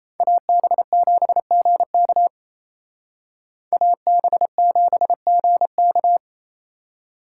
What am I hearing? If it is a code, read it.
Morse code, heard as a keyed tone, about 36 wpm, 700 Hz. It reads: AB7GK AB7GK